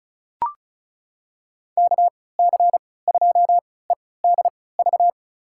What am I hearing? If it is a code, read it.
Morse code, heard as a keyed tone, about 36 wpm, spaced as for 20 wpm, 700 Hz. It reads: KC2EDV
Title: USA Call Signs - 6 Characters - 3x Character Spacing - 35wpm